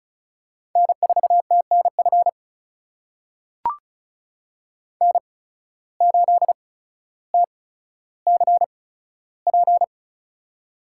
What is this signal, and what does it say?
Morse code, heard as a keyed tone, about 34 wpm, spaced as for 15 wpm, 700 Hz. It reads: N4TNF N8TCP
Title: USA Call Signs - 5 Characters - 8x Character Spacing - 35wpm